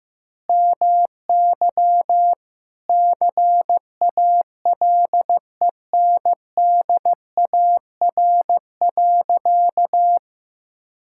Morse code, heard as a keyed tone, about 15 wpm, 700 Hz. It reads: MY CALENDAR.
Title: Sentences from Top 1600 Words 15wpm